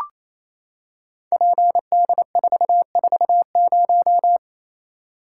Morse code, heard as a keyed tone, about 28 wpm, 700 Hz. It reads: PD440
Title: Code Groups - 5 Letter:Numbers - 28wpm